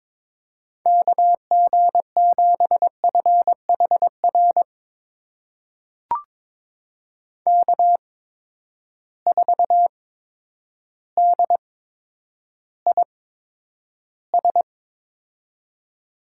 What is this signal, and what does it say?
Morse code, heard as a keyed tone, about 22 wpm, spaced as for 10 wpm, 700 Hz. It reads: KG7FHR K4DIS
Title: USA Call Signs - 8x Character Spacing - 22wpm